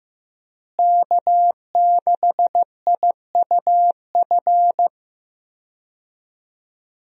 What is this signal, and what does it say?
Morse code, heard as a keyed tone, about 16 wpm, 700 Hz. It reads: K6IUF
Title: USA Call Signs - 5 Characters - Repeated 2x - 15wpm